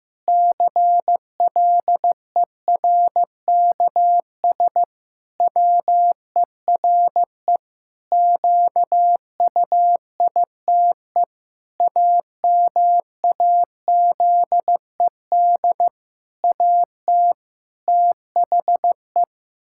Morse code, heard as a keyed tone, about 15 wpm, 700 Hz. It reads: CLERKS WERE QUITE AMAZED AT THE
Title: Pangram Challenge v2 - 15wpm